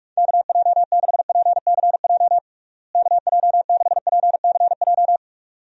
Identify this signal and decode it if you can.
Morse code, heard as a keyed tone, about 46 wpm, 700 Hz. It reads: KJ6PCJ KJ6PCJ
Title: USA Call Signs - 6 Characters - Repeated 2x - 45wpm